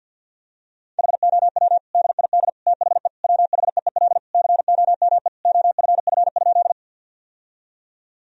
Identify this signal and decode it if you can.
Morse code, heard as a keyed tone, about 52 wpm, 700 Hz. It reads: HOW DID THE WHEEL COME OFF?
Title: Sentences from Top 1000 Words - Review Part 3 - 50wpm